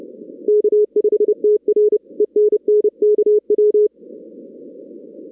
Morse code, heard as a keyed tone, about 30 wpm, 420 Hz. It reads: K5TR ENN KW
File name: 2016 arrl dx cw / 2016-ARRL-DX-CW-K5TR- - 0-17